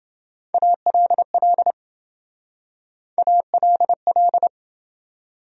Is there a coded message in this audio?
Yes: Morse code from 0.5 to 4.5 s, heard as a keyed tone, about 28 wpm, 700 Hz. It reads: ALL ALL